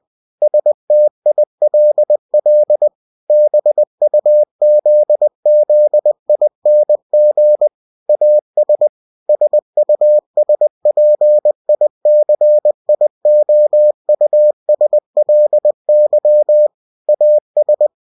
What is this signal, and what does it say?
Morse code, heard as a keyed tone, about 20 wpm, 600 Hz. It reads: STILL BUZZING AS SUSPICIOUSLY AS
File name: cw-books / winnie-the-pooh / winnie-20wpm-page-0009